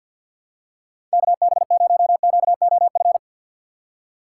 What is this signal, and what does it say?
Morse code, heard as a keyed tone, about 46 wpm, 700 Hz. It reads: KB0QOF
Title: USA Call Signs - 6x Character Spacing - 50wpm